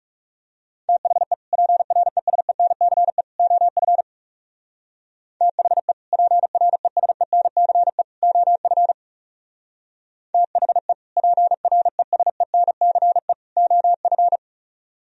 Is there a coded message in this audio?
Yes: Morse code from 0.9 to 14.4 s, heard as a keyed tone, about 38 wpm, 700 Hz. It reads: THE PRESENCE OF THE PRESENCE OF THE PRESENCE OF